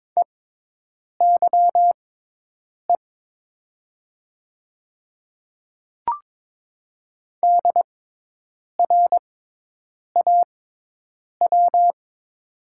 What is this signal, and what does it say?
Morse code, heard as a keyed tone, about 22 wpm, spaced as for 8 wpm, 700 Hz. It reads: EYE DRAW
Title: 6x Character Spacing - Top 300 Words - 22wpm